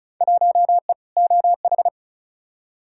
Morse code, heard as a keyed tone, about 34 wpm, 700 Hz. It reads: 1E OH
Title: Field Day Exchanges - 35wpm